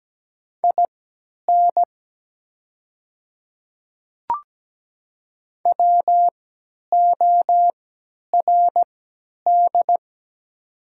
Morse code, heard as a keyed tone, about 17 wpm, spaced as for 10 wpm, 700 Hz. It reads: IN WORD
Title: 3x Character Spacing - Top 100 Words - 17wpm